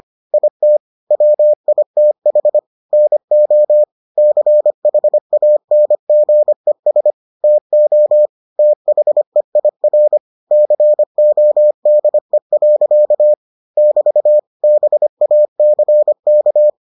Morse code, heard as a keyed tone, about 25 wpm, 600 Hz. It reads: IT WITH NO CHANGES TO THEIR CODE. <BT> BACK